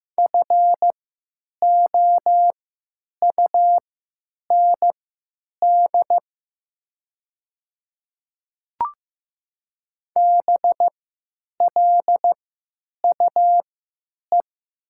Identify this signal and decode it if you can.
Morse code, heard as a keyed tone, about 15 wpm, spaced as for 9 wpm, 700 Hz. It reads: FOUND BLUE
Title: 3x Character Spacing - Top 500 Words - 15wpm